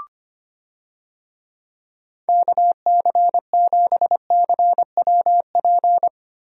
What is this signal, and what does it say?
Morse code, heard as a keyed tone, about 25 wpm, 700 Hz. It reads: KC7CWP
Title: USA Call Signs - 6 Characters - Repeated 3x - 25wpm